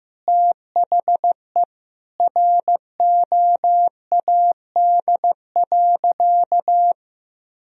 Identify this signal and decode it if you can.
Morse code, heard as a keyed tone, about 15 wpm, 700 Hz. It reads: THE ROAD.
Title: Sentences from Top 1000 Words - Review Part 2 - 15wpm